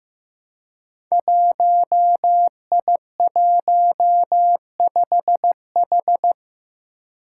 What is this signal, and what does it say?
Morse code, heard as a keyed tone, about 15 wpm, 700 Hz. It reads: 1I15H
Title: Five Challenging Letter-Number Combinations 15wpm